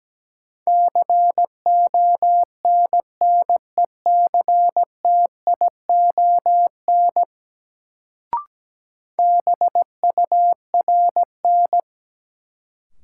Morse code, heard as a keyed tone, about 17 wpm, 700 Hz. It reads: CONNECTION BURN
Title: Sentences from Top 1300 Words - New Word Drill Alt - 17wpm